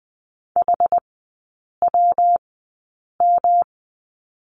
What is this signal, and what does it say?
Morse code, heard as a keyed tone, about 20 wpm, spaced as for 8 wpm, 700 Hz. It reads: HWM
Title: 114 Lesson 19e - Letter w - Three Characters 20wpm